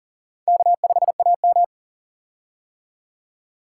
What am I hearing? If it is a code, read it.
Morse code, heard as a keyed tone, about 40 wpm, 700 Hz. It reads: K5AM
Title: K1USN SST QSO Practice - 40wpm